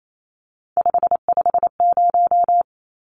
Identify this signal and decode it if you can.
Morse code, heard as a keyed tone, about 28 wpm, 700 Hz. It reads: 550
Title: Numbers - 3-Digit 28wpm